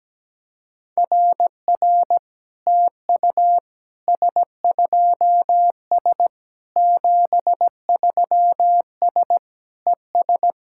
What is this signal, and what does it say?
Morse code, heard as a keyed tone, about 17 wpm, 700 Hz. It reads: RR TU S2S 73S ES
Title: SOTA QSO Practice 17wpm